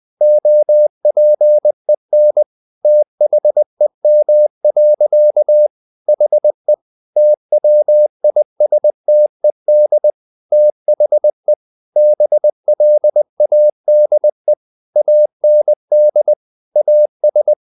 Morse code, heard as a keyed tone, about 20 wpm, 600 Hz. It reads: OPEN THEM. HE TWISTED THE BLADE AND AS
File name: the old man and the sea part 2 20wpm